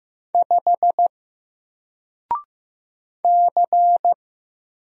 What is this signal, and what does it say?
Morse code, heard as a keyed tone, about 16 wpm, 700 Hz. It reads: 5 C
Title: Single Character (Letter:Number:Punctuation:Prosign) 15wpm